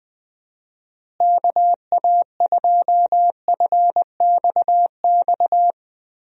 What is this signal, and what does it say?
Morse code, heard as a keyed tone, about 20 wpm, 700 Hz. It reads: KA2FXX
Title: USA Call Signs - 6 Characters - Repeated 3x - 20wpm